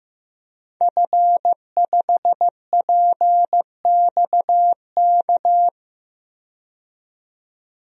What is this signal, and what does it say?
Morse code, heard as a keyed tone, about 15 wpm, 700 Hz. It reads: F5PXK
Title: DXCC-100 Call Signs - 5 Characters - 15wpm